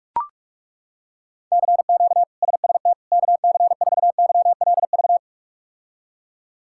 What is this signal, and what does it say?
Morse code, heard as a keyed tone, about 42 wpm, 700 Hz. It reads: CQ SST KC4YLV
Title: K1USN SST QSO Practice - 45wpm